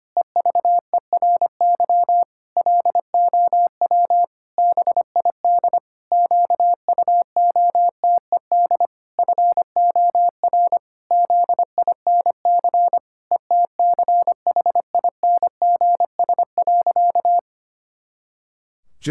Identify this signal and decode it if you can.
Morse code, heard as a keyed tone, about 25 wpm, 700 Hz. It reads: EVERY LOW BID QUOTED FOR ZINC ETCHINGS.
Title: Pangram Challenge v2 - 25wpm